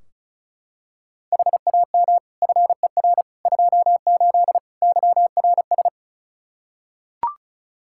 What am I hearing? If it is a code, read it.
Morse code, heard as a keyed tone, about 36 wpm, 700 Hz. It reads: HAM FER 28 YRS
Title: Practice QSO - Exchange 3 - More Info - 35wpm